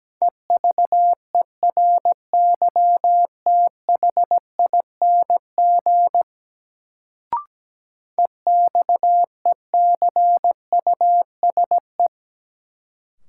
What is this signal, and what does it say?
Morse code, heard as a keyed tone, about 17 wpm, 700 Hz. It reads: EVERYTHING EXECUSE
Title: Sentences from Top 700 Words - New Word Drill - 17wpm